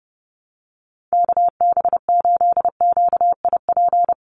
Morse code, heard as a keyed tone, about 30 wpm, 700 Hz. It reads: KB8QIP
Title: USA Call Signs 30wpm